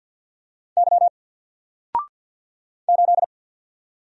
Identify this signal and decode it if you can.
Morse code, heard as a keyed tone, about 48 wpm, 700 Hz. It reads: Y 8
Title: Single Letter-Number 50wpm